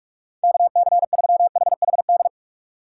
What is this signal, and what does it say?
Morse code, heard as a keyed tone, about 44 wpm, 690 Hz. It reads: KC3HHD
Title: USA Call Signs - 4x Character Spacing - 45wpm